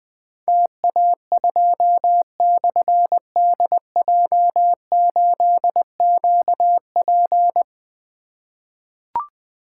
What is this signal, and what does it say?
Morse code, heard as a keyed tone, about 20 wpm, 700 Hz. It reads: TA2/DJ8QP
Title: Difficult Call Signs 20wpm